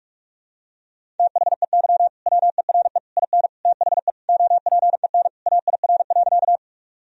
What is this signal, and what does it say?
Morse code, heard as a keyed tone, about 42 wpm, 700 Hz. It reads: THEY WERE IN THE OPEN AIR.
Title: Sentences from Top 1000 Words - Review Part 1 - 45wpm